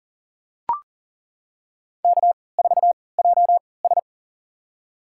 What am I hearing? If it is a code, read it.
Morse code, heard as a keyed tone, about 40 wpm, spaced as for 23 wpm, 700 Hz. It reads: K4JS
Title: USA Call Signs - 4 Characters - 3x Character Spacing - 40wpm